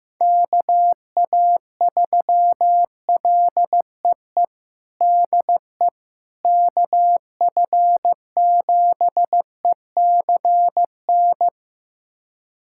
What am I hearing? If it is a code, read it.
Morse code, heard as a keyed tone, about 15 wpm, 700 Hz. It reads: KA3LEE DE KF7ECN
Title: USA Callsign DE Callsign - 6 Characters - 15wpm